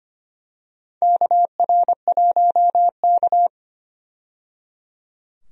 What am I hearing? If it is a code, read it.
Morse code, heard as a keyed tone, about 25 wpm, 700 Hz. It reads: KR1K
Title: USA Call Signs - 4 Characters - 25wpm